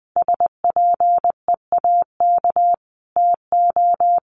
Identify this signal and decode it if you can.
Morse code, heard as a keyed tone, about 20 wpm, 700 Hz. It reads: SPEAK TO
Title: Sentences from Top 1100 Words 20wpm